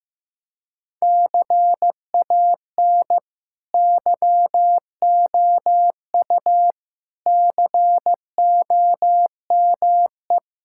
Morse code, heard as a keyed tone, about 15 wpm, 700 Hz. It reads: CAN YOU COME
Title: Sentences from Top 1000 Words - Review Part 5 - 15wpm